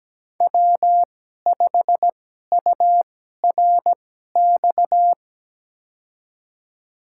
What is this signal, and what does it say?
Morse code, heard as a keyed tone, about 17 wpm, spaced as for 12 wpm, 700 Hz. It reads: W5URX
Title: USA Call Signs - 5 Characters - 2x Character Spacing - 17wpm